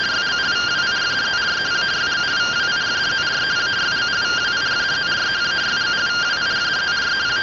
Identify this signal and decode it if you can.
Two-tone teleprinter signal, about 170 Hz shift.